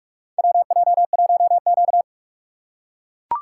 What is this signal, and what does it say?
Morse code, heard as a keyed tone, about 41 wpm, 700 Hz. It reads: WJ1Q
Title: USA Call Signs - 4 Characters - 3x Character Spacing - 45wpm